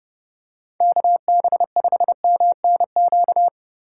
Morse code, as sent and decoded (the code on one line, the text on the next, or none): -.- -... ..... -- -. --.-
KB5MNQ